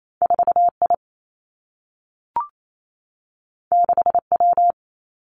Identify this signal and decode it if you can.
Morse code, heard as a keyed tone, about 28 wpm, 700 Hz. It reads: 4 I 6 W
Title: Two Challenging Letter-Number Combinations 28wpm